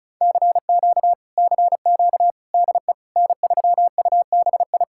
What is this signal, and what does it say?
Morse code, heard as a keyed tone, about 45 wpm, 700 Hz. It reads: CQ CQ DE N3UBI